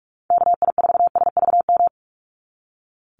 Morse code, heard as a keyed tone, about 44 wpm, 700 Hz. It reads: KI4SVM
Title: NA SOTA Activator Callsigns 45wpm